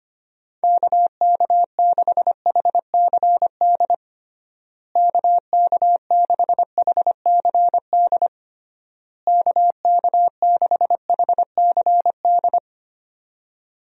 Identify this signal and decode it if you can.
Morse code, heard as a keyed tone, about 25 wpm, 700 Hz. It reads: KK6HCD KK6HCD KK6HCD
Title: USA Call Signs - 6 Characters - Repeated 3x - 25wpm